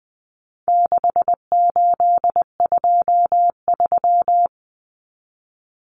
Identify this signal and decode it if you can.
Morse code, heard as a keyed tone, about 20 wpm, 700 Hz. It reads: 6823
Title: Numbers - 4-Digit 20wpm